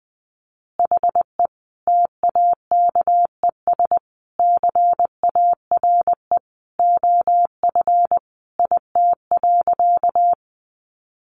Morse code, heard as a keyed tone, about 19 wpm, 700 Hz. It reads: HE TAKES CARE OF IT.
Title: Additional Easy Phrases 20wpm